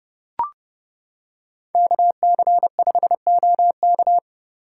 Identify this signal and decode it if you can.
Morse code, heard as a keyed tone, about 29 wpm, 700 Hz. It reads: KC5OK